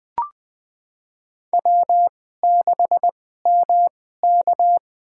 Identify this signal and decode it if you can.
Morse code, heard as a keyed tone, about 20 wpm, spaced as for 15 wpm, 700 Hz. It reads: W6MK
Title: USA Call Signs - 4 Characters - 2x Character Spacing - 20wpm